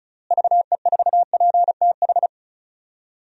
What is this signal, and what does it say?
Morse code, heard as a keyed tone, about 36 wpm, 700 Hz. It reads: VE4PTH
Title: DXCC-100 Call Signs - 6 Characters - Repeated 2x - 35wpm